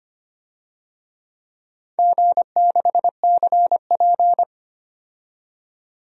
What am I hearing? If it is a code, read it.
Morse code, heard as a keyed tone, about 25 wpm, 700 Hz. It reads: G6CP
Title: DXCC-100 Call Signs - 4 Characters - 25wpm